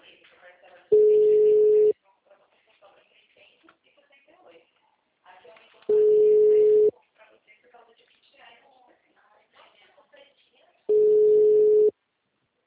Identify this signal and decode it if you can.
Ringback tone (Europe, 425 Hz), with speech.